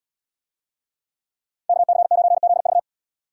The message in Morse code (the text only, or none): KC9ZV